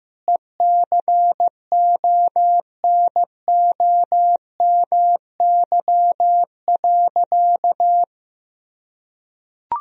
Morse code, heard as a keyed tone, about 15 wpm, 700 Hz. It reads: ECONOMY.
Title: Sentences from Top 1000 Words - Review Part 3 - 15wpm